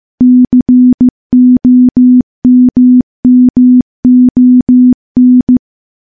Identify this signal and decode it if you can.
Morse code, heard as a keyed tone, about 15 wpm, 250 Hz. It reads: COMMON
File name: morse / morse-training-15WPM